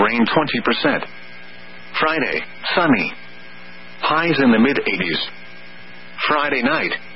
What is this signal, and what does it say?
Emergency Alert System (SAME) tones (header bursts), heard as harsh buzzing bursts, with speech. It is heard from 1.1 to 6.1 s.